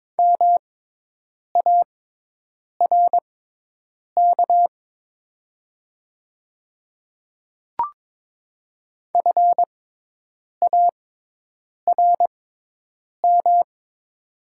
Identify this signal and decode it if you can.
Morse code, heard as a keyed tone, about 22 wpm, spaced as for 8 wpm, 700 Hz. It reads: MARK FARM
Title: 6x Character Spacing - Top 500 Words - 22wpm